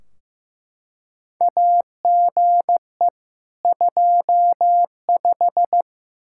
Morse code, heard as a keyed tone, about 15 wpm, 700 Hz. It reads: AGE 25
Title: Practice QSO - Exchange 3 - More Info - 15wpm